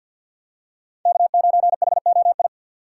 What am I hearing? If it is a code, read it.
Morse code, heard as a keyed tone, about 53 wpm, 700 Hz. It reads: K9HOI
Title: USA Call Signs - 5 Characters - 6x Character Spacing - 50wpm